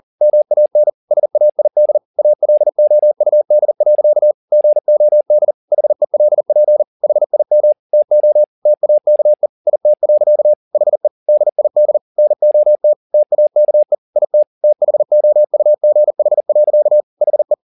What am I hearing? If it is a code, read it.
Morse code, heard as a keyed tone, about 40 wpm, 600 Hz. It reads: MAN SAID ALOUD. GOD HELP HIM TO TAKE IT. HE DID NOT TAKE IT THOUGH. HE